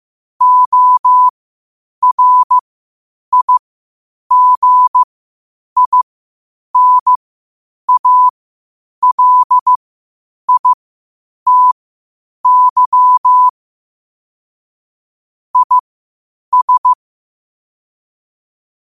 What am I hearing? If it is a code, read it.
Morse code, heard as a keyed tone, about 15 wpm, spaced as for 8 wpm, 1000 Hz. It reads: ORIGINALITY IS
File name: qotdfiles / 8wpm